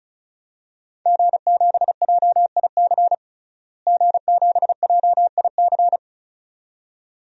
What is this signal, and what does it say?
Morse code, heard as a keyed tone, about 34 wpm, 700 Hz. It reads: G7JIC G7JIC
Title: DXCC-100 Call Signs - 5 Characters - Repeated 2x - 35wpm